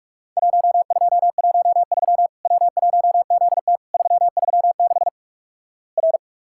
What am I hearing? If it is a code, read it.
Morse code, heard as a keyed tone, about 48 wpm, 700 Hz. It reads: 1213 W1ZT 336 R
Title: Worked All Europe DX Contest - QTC Exchange - 45wpm